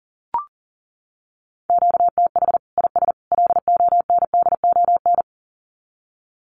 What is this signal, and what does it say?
Morse code, heard as a keyed tone, about 40 wpm, 700 Hz. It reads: QTH IS LONDON
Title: Common QSO Phrases 40wpm